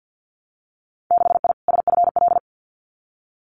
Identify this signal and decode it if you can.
Morse code, heard as a keyed tone, about 50 wpm, 700 Hz. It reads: 6I SFL
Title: Winter Field Day Exchanges Encore 50wpm